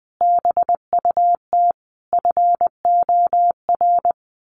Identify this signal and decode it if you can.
Morse code, heard as a keyed tone, about 20 wpm, 700 Hz. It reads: BUT FOR
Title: Sentences from Top 1400 Words 20wpm